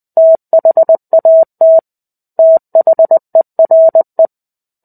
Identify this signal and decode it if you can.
Morse code, heard as a keyed tone, about 20 wpm, 650 Hz. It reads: THAT THERE
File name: 05-Chapter-05 20 WPM